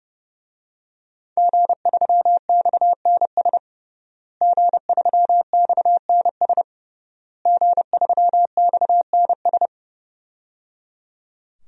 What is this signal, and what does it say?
Morse code, heard as a keyed tone, about 30 wpm, 700 Hz. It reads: G3XNS G3XNS G3XNS
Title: DXCC-100 Call Signs - 5 Characters - Repeated 3x - 30wpm